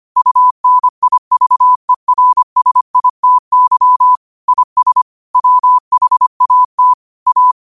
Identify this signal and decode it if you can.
Morse code, heard as a keyed tone, about 25 wpm, 1000 Hz. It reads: UNIVERSITY IS WHAT A
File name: qotdfiles / 25wpm